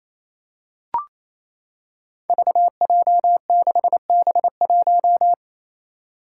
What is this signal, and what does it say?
Morse code, heard as a keyed tone, about 27 wpm, 700 Hz. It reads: VJ6B1
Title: Five Challenging Letter-Number Combinations 28wpm